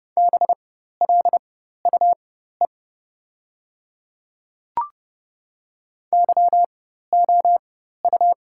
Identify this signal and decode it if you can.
Morse code, heard as a keyed tone, about 30 wpm, spaced as for 14 wpm, 700 Hz. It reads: BLUE YOU
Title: 4x Character Spacing - Top 500 Words - 30wpm